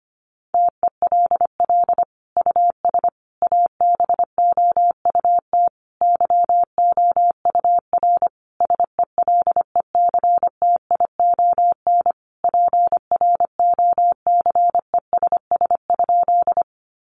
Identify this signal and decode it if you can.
Morse code, heard as a keyed tone, about 25 wpm, 700 Hz. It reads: TELL US ABOUT YOUR SELECTION PROCESS?
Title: Sentences from Top 1400 Words 25wpm